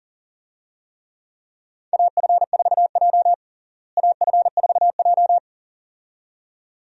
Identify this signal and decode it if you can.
Morse code, heard as a keyed tone, about 40 wpm, 700 Hz. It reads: AF4J AF4J
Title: USA Call Signs - 4 Characters - Repeated 2x - 40wpm